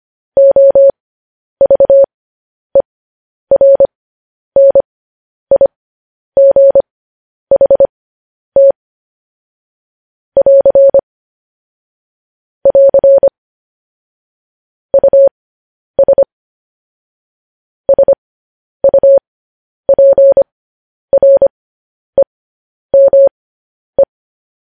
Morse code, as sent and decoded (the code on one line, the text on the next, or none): --- ...- . .-. -. .. --. .... - / .-.-. / .-.-. / ..- ... / ... ..- .--. .-. -- .
OVERNIGHT <AR> <AR> US SUPREME